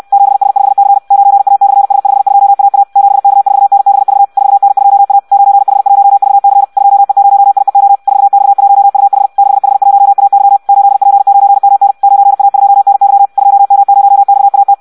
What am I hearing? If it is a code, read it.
Morse code, heard as a keyed tone, about 64 wpm, 800 Hz. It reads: 7S5C 9A/S52AA 7K4NBV 4N2A 9H1VC 2E0IEO 4X1UH 6H1AY 8P9MN 9A2NY 2M0XAT
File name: Audio practica CW / qrq-tx-66 2